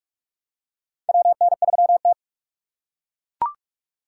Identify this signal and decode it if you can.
Morse code, heard as a keyed tone, about 43 wpm, 700 Hz. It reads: WN3T
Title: USA Call Signs - 4 Characters - 2x Character Spacing - 45wpm